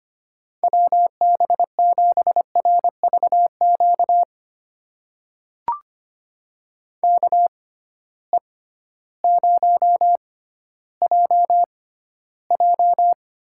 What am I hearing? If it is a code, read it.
Morse code, heard as a keyed tone, about 25 wpm, spaced as for 14 wpm, 700 Hz. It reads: WB7RVQ KE0JJ
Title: USA Call Signs - 6 Characters - 6x Character Spacing - 25wpm